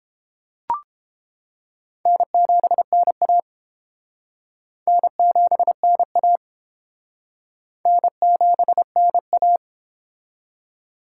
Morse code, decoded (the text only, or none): N7NA N7NA N7NA